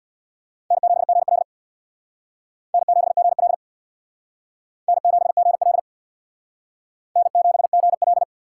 Morse code, heard as a keyed tone, about 61 wpm, 700 Hz. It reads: N7GL N7GL N7GL N7GL
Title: Speed Racing - USA 1x2 Call Signs 50wpm